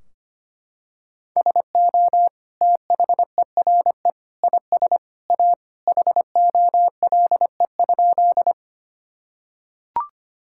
Morse code, heard as a keyed tone, about 25 wpm, 700 Hz. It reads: SO THERE IS A HOLE?
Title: Sentences from Top 1600 Words 25wpm